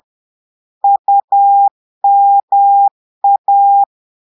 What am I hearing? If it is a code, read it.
Morse code, heard as a keyed tone, about 10 wpm, 800 Hz. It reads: UMA
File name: portugais texte en clair 2 vitesse 10 mots minutes